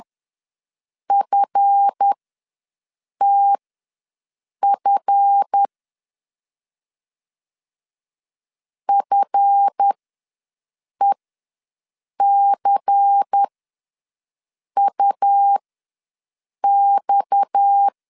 Morse code, heard as a keyed tone, about 10 wpm, spaced as for 6 wpm, 800 Hz. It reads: FTF FECUX
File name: LEZIONI / 7 GF9Z / 7 GF ws 10wpm 3918